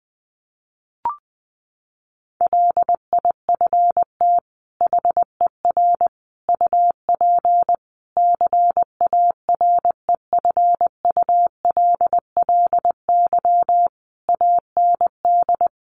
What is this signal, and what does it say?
Morse code, heard as a keyed tone, about 20 wpm, 700 Hz. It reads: LIFT HER UP CAREFULLY AND
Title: Sentences from Top 1300 Words 20wpm